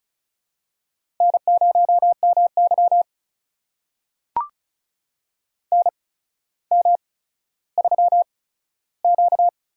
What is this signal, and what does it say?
Morse code, heard as a keyed tone, about 36 wpm, spaced as for 15 wpm, 700 Hz. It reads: N0MY NM3Q
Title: USA Call Signs - 4 Characters - 8x Character Spacing - 35wpm